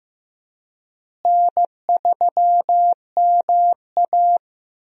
Morse code, heard as a keyed tone, about 15 wpm, 700 Hz. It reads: N3MA
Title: USA Call Signs - 4 Characters - Repeated 3x - 15wpm